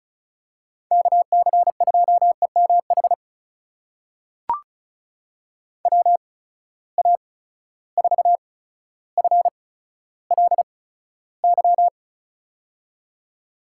Morse code, heard as a keyed tone, about 36 wpm, spaced as for 15 wpm, 700 Hz. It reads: KC2EMH WA4FLY
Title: USA Call Signs - 6 Characters - 8x Character Spacing - 35wpm